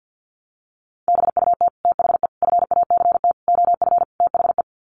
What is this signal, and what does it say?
Morse code, heard as a keyed tone, about 50 wpm, 700 Hz. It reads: BUT THE FACT OF THE
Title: Sets of 5 Words Encore2 50wpm